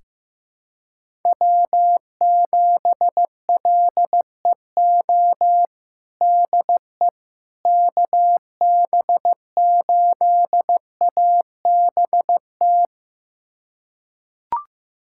Morse code, heard as a keyed tone, about 15 wpm, 700 Hz. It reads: W7LEO DE KB8ABT
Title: USA Callsign DE Callsign 15wpm